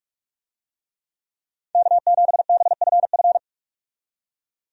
Morse code, heard as a keyed tone, about 41 wpm, 700 Hz. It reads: K7BFF
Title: USA Call Signs - 5 Characters - 45wpm